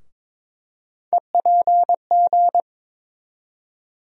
Morse code, heard as a keyed tone, about 22 wpm, 700 Hz. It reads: EPG
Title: USA Call Sign Postfixes - 3 Letters 22wpm